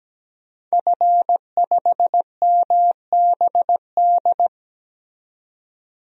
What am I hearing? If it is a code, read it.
Morse code, heard as a keyed tone, about 17 wpm, 700 Hz. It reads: F5MBD
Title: DXCC-100 Call Signs - 5 Characters - 17wpm